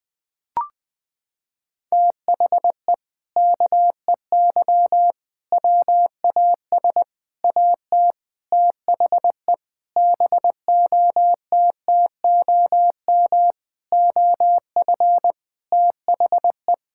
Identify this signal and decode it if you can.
Morse code, heard as a keyed tone, about 20 wpm, 700 Hz. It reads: THE KEY WAS AT THE BOTTOM OF THE
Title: Sentences with 4-Word Phrases - 20wpm